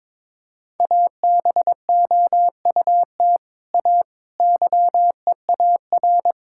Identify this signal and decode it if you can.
Morse code, heard as a keyed tone, about 21 wpm, 700 Hz. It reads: ABOUT A YEAR
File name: practice / 22wpm / Sets of 7 Words - 22wpm